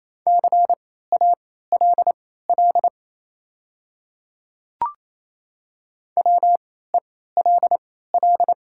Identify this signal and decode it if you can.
Morse code, heard as a keyed tone, about 28 wpm, spaced as for 16 wpm, 700 Hz. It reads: CALL WELL
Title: 3x Character Spacing - Top 200 Words - 28wpm